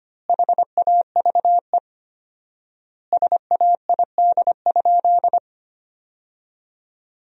Morse code, heard as a keyed tone, about 25 wpm, 700 Hz. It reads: HAVE SAID?
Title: 4x Word-Spacing - Sentences from Top 100 Words - 25wpm